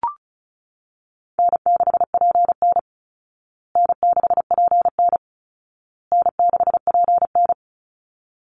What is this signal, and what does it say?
Morse code, heard as a keyed tone, about 34 wpm, 700 Hz. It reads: N6PN N6PN N6PN
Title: Easy Call Signs Repeated 3x - 35wpm